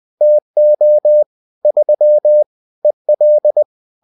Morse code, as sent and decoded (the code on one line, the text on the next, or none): - --- / ...-- / . .-..
TO 3 EL